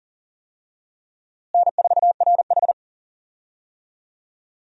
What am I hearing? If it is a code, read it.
Morse code, heard as a keyed tone, about 40 wpm, 700 Hz. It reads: N4RH
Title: USA Call Signs - 4 Characters - Repeated 2x - 40wpm